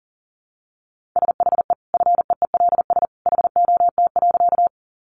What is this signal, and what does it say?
Morse code, heard as a keyed tone, about 40 wpm, 700 Hz. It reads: SHE FEELS HOT.
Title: Additional Easy Phrases 40wpm